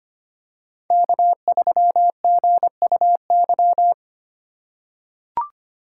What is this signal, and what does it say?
Morse code, heard as a keyed tone, about 25 wpm, 700 Hz. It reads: K3GUY
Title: USA Call Signs - 5 Characters - 4x Character Spacing - 25wpm